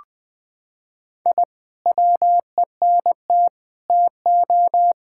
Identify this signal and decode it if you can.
Morse code, heard as a keyed tone, about 20 wpm, 700 Hz. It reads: I WENT TO
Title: Sentences from Top 1000 Words - Review Part 6 - 20wpm